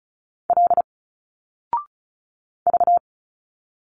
Morse code, heard as a keyed tone, about 34 wpm, 700 Hz. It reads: L V
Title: Single Letters 35wpm